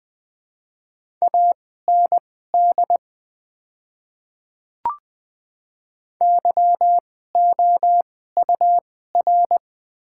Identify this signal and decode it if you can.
Morse code, heard as a keyed tone, about 21 wpm, spaced as for 15 wpm, 700 Hz. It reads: AND YOUR
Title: 2x Character Spacing - Top 300 Words - 20wpm